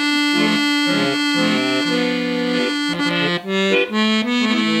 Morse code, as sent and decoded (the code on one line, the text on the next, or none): none